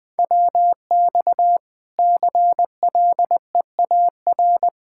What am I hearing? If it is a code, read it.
Morse code, heard as a keyed tone, about 20 wpm, 700 Hz. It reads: WX CLEAR
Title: Practice QSO - Exchange 2 - Info - 20wpm